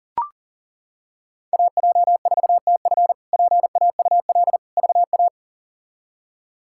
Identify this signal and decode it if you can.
Morse code, heard as a keyed tone, about 40 wpm, 690 Hz. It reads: AJ4TF PAUL VA
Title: K1USN SST QSO Practice - 40wpm